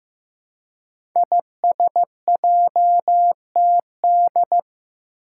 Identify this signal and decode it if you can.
Morse code, heard as a keyed tone, about 15 wpm, 700 Hz. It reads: ISJTD
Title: Five Challenging Letter-Number Combinations 15wpm